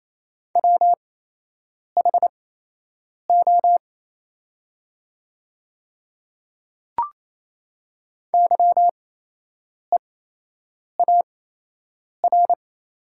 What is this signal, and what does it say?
Morse code, heard as a keyed tone, about 28 wpm, spaced as for 8 wpm, 700 Hz. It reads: WHO YEAR
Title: 8x Character Spacing - Top 300 Words - 28wpm